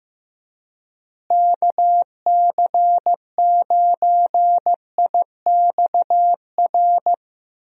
Morse code, heard as a keyed tone, about 15 wpm, 700 Hz. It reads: KC9IXR
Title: USA Call Signs - 6 Characters - Repeated 3x - 15wpm